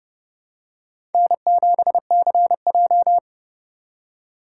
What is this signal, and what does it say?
Morse code, heard as a keyed tone, about 28 wpm, 700 Hz. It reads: N7CJ